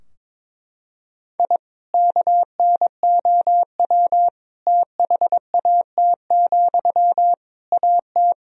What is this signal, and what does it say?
Morse code, heard as a keyed tone, about 22 wpm, 700 Hz. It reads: I KNOW THAT, AT
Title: Sentences from Top 1000 Words - Review Part 4 - 22wpm